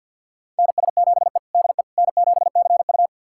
Morse code, heard as a keyed tone, about 48 wpm, 690 Hz. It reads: NS7E DE N7CU
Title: USA Callsign DE Callsign - 4 Characters - 50wpm